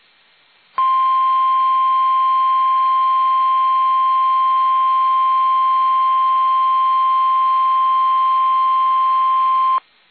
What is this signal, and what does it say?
Emergency Alert System (SAME) tones (the 1050 Hz attention tone). It is heard from 0.7 to 9.8 s.
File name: eas / KUSO / KUSO